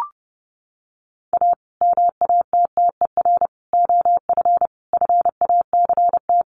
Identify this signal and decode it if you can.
Morse code, heard as a keyed tone, about 30 wpm, 700 Hz. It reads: A MATTER OF FACT
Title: Sets of 4 Words 30wpm